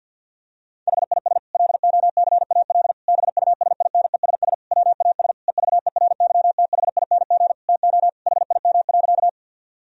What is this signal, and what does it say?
Morse code, heard as a keyed tone, about 50 wpm, 700 Hz. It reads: HIS LOCAL BUSINESS WAS EVERYTHING TO HIM.
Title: Sentences from Top 1000 Words - Review Part 3 - 50wpm